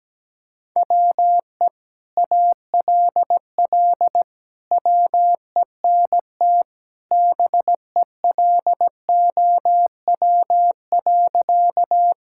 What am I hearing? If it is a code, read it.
Morse code, heard as a keyed tone, about 17 wpm, 700 Hz. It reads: WE ALL WENT BELOW.
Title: Sentences from Top 1000 Words - Review Part 5 - 17wpm